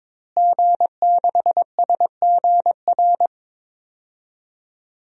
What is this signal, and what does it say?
Morse code, heard as a keyed tone, about 22 wpm, 700 Hz. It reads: G6SGR
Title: DXCC-100 Call Signs - 5 Characters - Repeated 3x - 22wpm